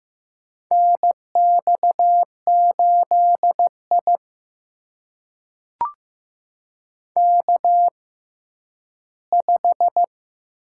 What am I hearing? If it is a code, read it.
Morse code, heard as a keyed tone, about 15 wpm, spaced as for 10 wpm, 700 Hz. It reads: NX8I K5
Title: USA Call Signs - 4 Characters - 6x Character Spacing - 15wpm